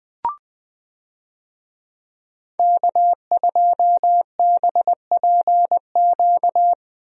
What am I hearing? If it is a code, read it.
Morse code, heard as a keyed tone, about 21 wpm, 700 Hz. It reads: K2BPQ
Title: USA Call Signs - 5 Characters - 20wpm